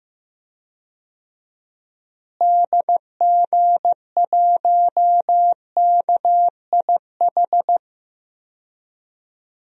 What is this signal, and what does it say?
Morse code, heard as a keyed tone, about 15 wpm, 700 Hz. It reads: DG1KIH
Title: DXCC-100 Call Signs - 6 Characters - Repeated 3x - 15wpm